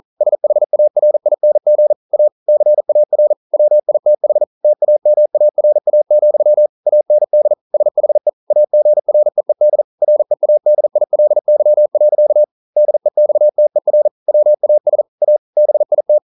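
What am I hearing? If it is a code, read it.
Morse code, heard as a keyed tone, about 42 wpm, 600 Hz. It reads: SHARING A CAR WITH TAMARA, AND SHE AGREED READILY. DEXTER WAS A BIT
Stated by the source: Morse code practice at 41 wpm, from CW med Gnister.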